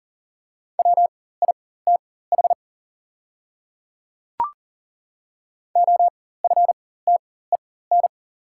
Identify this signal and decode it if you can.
Morse code, heard as a keyed tone, about 40 wpm, spaced as for 19 wpm, 690 Hz. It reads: WITH OFTEN
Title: 4x Character Spacing - Top 500 Words - 40wpm